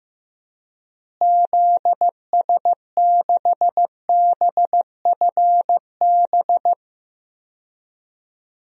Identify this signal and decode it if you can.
Morse code, heard as a keyed tone, about 15 wpm, 700 Hz. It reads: ZS6BFB
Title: DXCC-100 Call Signs - 6 Characters - Repeated 2x - 15wpm